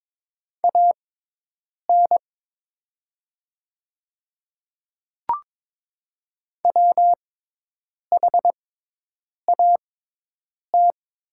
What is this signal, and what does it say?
Morse code, heard as a keyed tone, about 22 wpm, spaced as for 8 wpm, 700 Hz. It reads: AN WHAT
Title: 6x Character Spacing - Top 100 Words - 22wpm